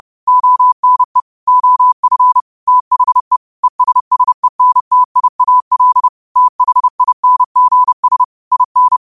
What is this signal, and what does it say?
Morse code, heard as a keyed tone, about 30 wpm, 1000 Hz. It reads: ONE OF THE ESSENTIAL THINGS IN